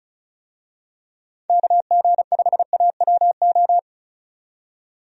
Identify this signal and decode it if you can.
Morse code, heard as a keyed tone, about 34 wpm, 700 Hz. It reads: KG5AWO
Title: USA Call Signs - 6 Characters - 35wpm